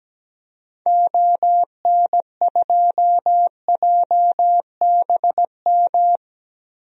Morse code, heard as a keyed tone, about 17 wpm, 700 Hz. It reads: ON2JBM
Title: DXCC-100 Call Signs - 6 Characters - 17wpm